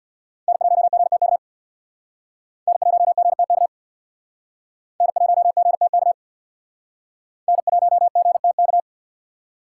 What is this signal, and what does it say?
Morse code, heard as a keyed tone, about 62 wpm, 700 Hz. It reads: N1GTK N1GTK N1GTK N1GTK
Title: Speed Racing - USA 1x3 Call Signs 50wpm